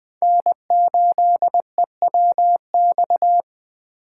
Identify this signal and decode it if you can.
Morse code, heard as a keyed tone, about 20 wpm, 700 Hz. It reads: N8EWX
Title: K1USN SST QSO Practice - 20wpm